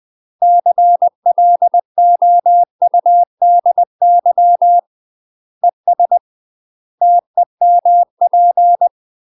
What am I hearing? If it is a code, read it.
Morse code, heard as a keyed tone, about 20 wpm, 700 Hz. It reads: CLOUDY ES TEMP
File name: UK MASTER QSOs TRAINING / 20WPM / UK MASTER QSOs 0001